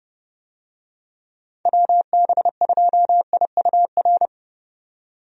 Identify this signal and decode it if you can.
Morse code, heard as a keyed tone, about 29 wpm, 700 Hz. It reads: WB2IUR
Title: USA Call Signs - 6 Characters - 30wpm